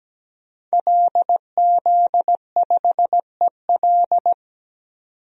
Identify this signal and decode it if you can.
Morse code, heard as a keyed tone, about 17 wpm, 700 Hz. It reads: LZ5EL
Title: DXCC-100 Call Signs - 5 Characters - 17wpm